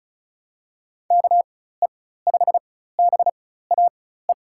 Morse code, heard as a keyed tone, about 36 wpm, spaced as for 16 wpm, 700 Hz. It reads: KE5BAE
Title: USA Call Signs - 4x Character Spacing - 35wpm